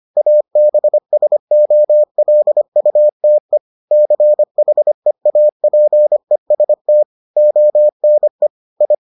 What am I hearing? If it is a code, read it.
Morse code, heard as a keyed tone, about 25 wpm, 600 Hz. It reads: ABSOLUTE CHEAPEST ONE I